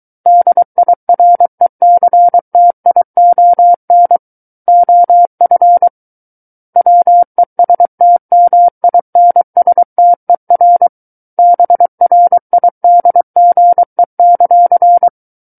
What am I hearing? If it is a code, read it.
Morse code, heard as a keyed tone, about 23 wpm, 700 Hz. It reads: DIRECTION OF WESTMINSTER BRIDGE;
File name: Book One / 14-Chapter 14 - 23 WPM